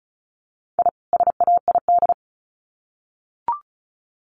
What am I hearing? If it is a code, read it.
Morse code, heard as a keyed tone, about 34 wpm, 700 Hz. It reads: I SAID
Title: Sets of 2 Words 35wpm